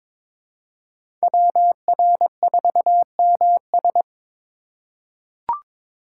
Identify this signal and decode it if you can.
Morse code, heard as a keyed tone, about 21 wpm, 700 Hz. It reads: WR4MS
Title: USA Call Signs - 5 Characters - 2x Character Spacing - 22wpm